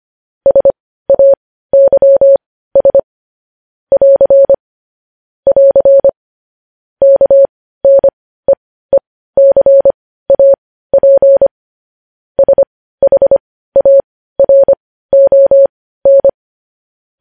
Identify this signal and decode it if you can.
Morse code, heard as a keyed tone, about 25 wpm, spaced as for 15 wpm, 550 Hz. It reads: SAYS <AR> <AR> KNEECAP SHARON
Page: news headlines in morse code at 15 wpm